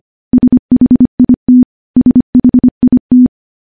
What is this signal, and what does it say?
Morse code, heard as a keyed tone, about 25 wpm, 250 Hz. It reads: SHIT SHIT
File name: morse / morse-training-25WPM